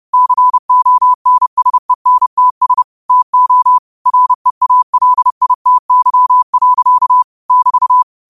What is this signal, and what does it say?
Morse code, heard as a keyed tone, about 30 wpm, 1000 Hz. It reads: CONSENTS TO REALITY. <BT>
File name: 30wpm